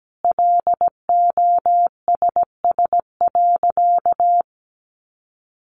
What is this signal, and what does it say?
Morse code, heard as a keyed tone, about 17 wpm, 700 Hz. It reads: LOSS.
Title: Sentences from Top 1400 Words 17wpm